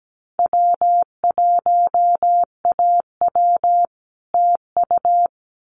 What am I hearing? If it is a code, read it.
Morse code, heard as a keyed tone, about 17 wpm, 700 Hz. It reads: W1AW TU
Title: Common QSO Phrases Encore 17wpm